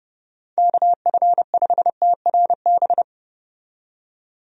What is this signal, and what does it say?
Morse code, heard as a keyed tone, about 32 wpm, 700 Hz. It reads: KF5TRB
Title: USA Call Signs - 6 Characters - 30wpm